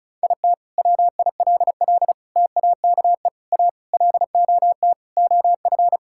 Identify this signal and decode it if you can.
Morse code, heard as a keyed tone, about 36 wpm, 700 Hz. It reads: IT WILL TAKE A LOT OF
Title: Sentences from Top 1000 Words - Review Part 6 - 35wpm